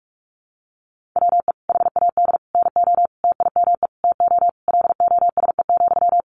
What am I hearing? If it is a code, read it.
Morse code, heard as a keyed tone, about 44 wpm, 700 Hz. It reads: WE HAD NO TIME TO LOSE,
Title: Sentences from Top 1100 Words 45wpm